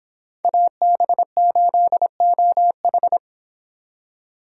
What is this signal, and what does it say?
Morse code, heard as a keyed tone, about 26 wpm, 700 Hz. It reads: AB8OH